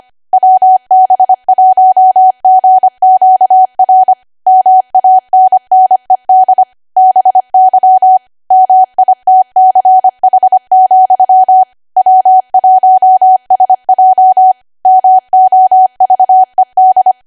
Morse code, heard as a keyed tone, about 25 wpm, 750 Hz. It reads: WB1GQR MANNED BY MITCH, W1SJ MOVED